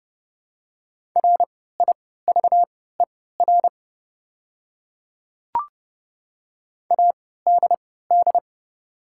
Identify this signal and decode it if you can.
Morse code, heard as a keyed tone, about 30 wpm, spaced as for 17 wpm, 700 Hz. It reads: RIVER ADD
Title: 3x Character Spacing - Top 300 Words - 30wpm